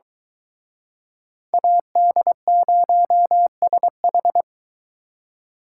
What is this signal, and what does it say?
Morse code, heard as a keyed tone, about 23 wpm, 700 Hz. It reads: AD0SH